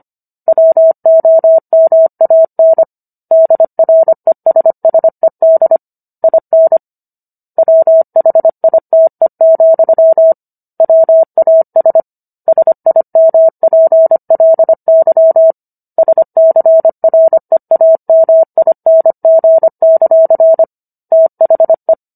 Morse code, heard as a keyed tone, about 25 wpm, 650 Hz. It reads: WOMAN DRESSED IN WHITE, WAS SIMPLY SCREAMING; THE